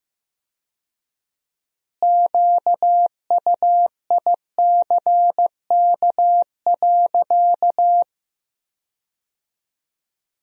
Morse code, heard as a keyed tone, about 15 wpm, 700 Hz. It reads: QUICK.
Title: 4x Word-Spacing - Sentences from Top 500 Words - 15wpm